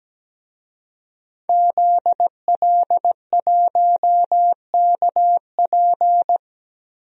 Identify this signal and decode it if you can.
Morse code, heard as a keyed tone, about 17 wpm, 700 Hz. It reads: ZL1KP
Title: DXCC-100 Call Signs - 5 Characters - Repeated 3x - 17wpm